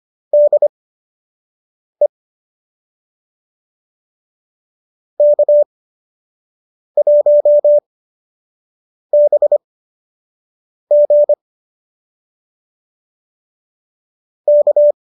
Morse code, heard as a keyed tone, about 25 wpm, spaced as for 6 wpm, 590 Hz. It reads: DE K1BG K